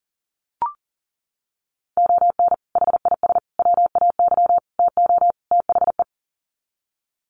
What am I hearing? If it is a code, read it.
Morse code, heard as a keyed tone, about 40 wpm, 700 Hz. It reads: ON HIS WAY TO THE
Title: Sets of 5 Words Encore2 40wpm